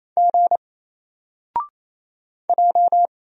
Morse code, heard as a keyed tone, about 27 wpm, 700 Hz. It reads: G J
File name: Single-Letter-Number-28wpm